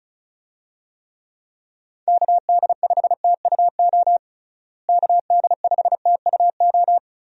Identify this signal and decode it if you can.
Morse code, heard as a keyed tone, about 36 wpm, 700 Hz. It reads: KD5TUO KD5TUO
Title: USA Call Signs - 6 Characters - Repeated 2x - 35wpm